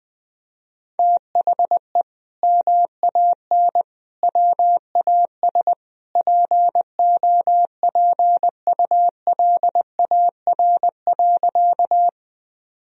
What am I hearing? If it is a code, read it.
Morse code, heard as a keyed tone, about 20 wpm, 700 Hz. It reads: THE MAN WAS POPULAR.
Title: Sentences from Top 1000 Words - Review Part 3 - 20wpm